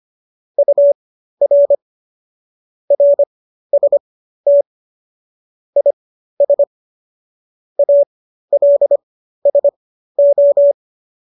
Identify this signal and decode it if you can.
Morse code, heard as a keyed tone, about 25 wpm, spaced as for 13 wpm, 570 Hz. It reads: UR RST IS ALSO